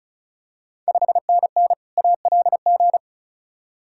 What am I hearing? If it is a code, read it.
Morse code, heard as a keyed tone, about 36 wpm, 700 Hz. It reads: 5NN ALG
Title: Ontario QSO Party Exchange 35wpm